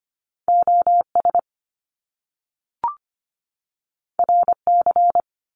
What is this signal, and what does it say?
Morse code, heard as a keyed tone, about 25 wpm, 700 Hz. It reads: OS RC
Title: Top Two Letter Combinations in Words 25wpm